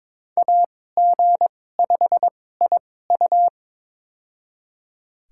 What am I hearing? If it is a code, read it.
Morse code, heard as a keyed tone, about 21 wpm, spaced as for 16 wpm, 700 Hz. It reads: AG5IU
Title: USA Call Signs - 5 Characters - 2x Character Spacing - 22wpm